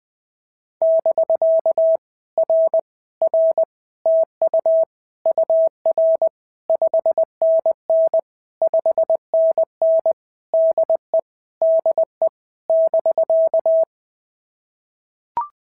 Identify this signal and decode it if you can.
Morse code, heard as a keyed tone, about 19 wpm, 650 Hz. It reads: <BK> R R TU UR 5NN 5NN DE DE <BK>
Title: POTA QSO Practice - 20wpm